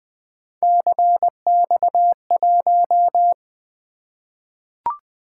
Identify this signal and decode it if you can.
Morse code, heard as a keyed tone, about 21 wpm, 700 Hz. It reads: CX1
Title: Three Letter-Number Combinations - 20wpm